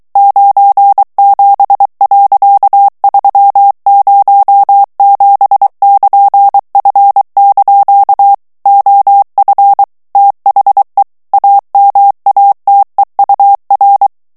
Morse code, heard as a keyed tone, about 23 wpm, 800 Hz. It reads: 97.307<KN>F) OF THE AMATEUR
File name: ANGLAIS texte clair 3 vitesse 22 mots minute